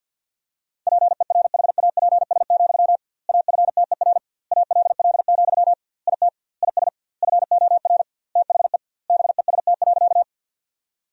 Morse code, heard as a keyed tone, about 53 wpm, 700 Hz. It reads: PERHAPS, AFTER ALL, IT IS FOR THE BEST.
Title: 2x Word-Spacing - Sentences from Top 500 Words - 50wpm